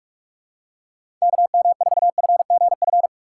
Morse code, heard as a keyed tone, about 43 wpm, 700 Hz. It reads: KM4FGF
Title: USA Call Signs - 6 Characters - 6x Character Spacing - 45wpm